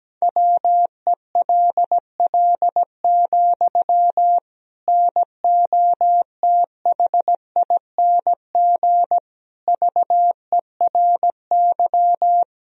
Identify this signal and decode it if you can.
Morse code, heard as a keyed tone, about 17 wpm, 700 Hz. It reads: WELL, NOTHING VERY
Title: Sentences from Top 1000 Words - Review Part 3 - 17wpm